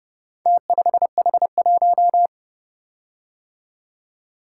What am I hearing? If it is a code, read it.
Morse code, heard as a keyed tone, about 29 wpm, 700 Hz. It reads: T5H1